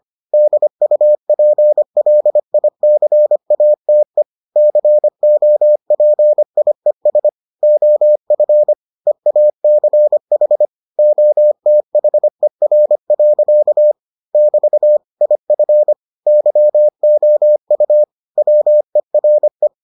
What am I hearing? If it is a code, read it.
Morse code, heard as a keyed tone, about 25 wpm, 600 Hz. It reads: DUPLICATE COPIES OF EACH OTHER. <BT> IF YOU WERE